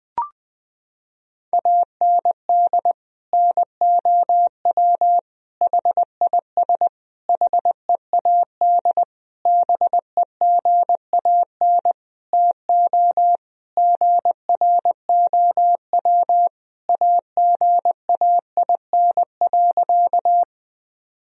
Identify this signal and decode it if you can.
Morse code, heard as a keyed tone, about 21 wpm, 700 Hz. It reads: AND NOW HIS HEAD BEGAN TO GROW AGAIN.
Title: Sentences from Top 1000 Words - Review Part 1 - 20wpm